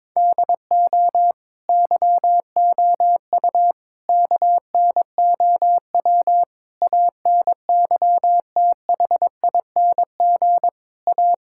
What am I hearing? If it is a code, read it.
Morse code, heard as a keyed tone, about 22 wpm, 700 Hz. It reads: DO YOU KNOW ANYTHING A
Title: Sentences from Top 1600 Words 22wpm